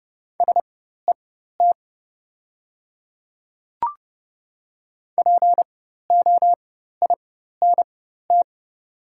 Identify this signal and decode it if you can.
Morse code, heard as a keyed tone, about 30 wpm, spaced as for 14 wpm, 700 Hz. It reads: SET POINT
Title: 4x Character Spacing - Top 300 Words - 30wpm